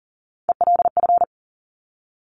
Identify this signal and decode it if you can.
Morse code, heard as a keyed tone, about 40 wpm, 700 Hz. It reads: ELF
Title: Top Three Letter Combinations in Words 40wpm